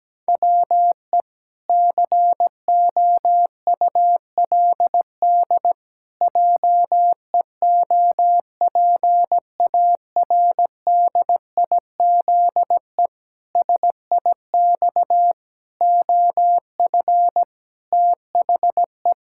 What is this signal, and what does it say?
Morse code, heard as a keyed tone, about 17 wpm, 700 Hz. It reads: WE COULD JEOPARDIZE SIX OF THE
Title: Pangram Challenge v2 - 17wpm